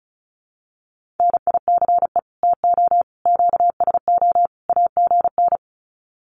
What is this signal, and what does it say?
Morse code, heard as a keyed tone, about 34 wpm, 700 Hz. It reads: NICE TO QSO AGN
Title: Common QSO Phrases 35wpm